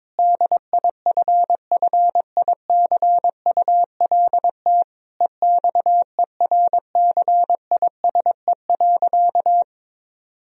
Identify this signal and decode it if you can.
Morse code, heard as a keyed tone, about 22 wpm, 700 Hz. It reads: DIFFICULT EXERCISE.
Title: Sentences from Top 1000 Words - Review Part 4 - 22wpm